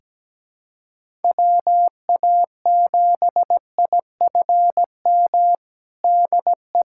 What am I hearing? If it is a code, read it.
Morse code, heard as a keyed tone, about 17 wpm, 700 Hz. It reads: WA7IFM DE
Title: USA Callsign DE Callsign - 6 Characters - 17wpm